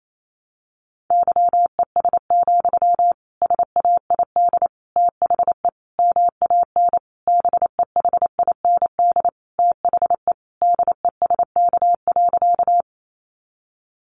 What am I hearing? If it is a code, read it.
Morse code, heard as a keyed tone, about 28 wpm, 700 Hz. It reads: YES, SAID THE MAN BEHIND THE DESK.
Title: A Sound Of Thunder 28wpm